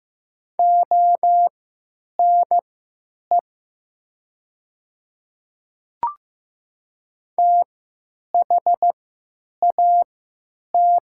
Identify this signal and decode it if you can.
Morse code, heard as a keyed tone, about 15 wpm, spaced as for 9 wpm, 690 Hz. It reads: ONE THAT
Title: 3x Character Spacing - Top 100 Words - 15wpm